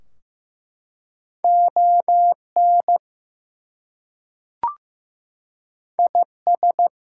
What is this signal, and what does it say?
Morse code, heard as a keyed tone, about 15 wpm, 700 Hz. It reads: ON IS
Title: Two Letter Words - 15wpm